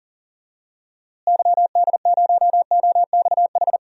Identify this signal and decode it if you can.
Morse code, heard as a keyed tone, about 38 wpm, 700 Hz. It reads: YD0OXH
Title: DXCC-100 Call Signs - 6 Characters - Repeated 2x - 40wpm